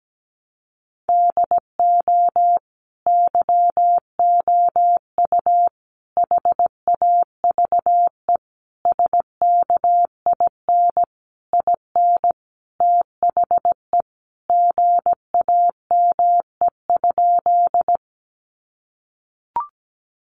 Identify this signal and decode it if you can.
Morse code, heard as a keyed tone, about 17 wpm, 700 Hz. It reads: DO YOU HAVE SKIN IN THE GAME?
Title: Sentences from Top 1300 Words 17wpm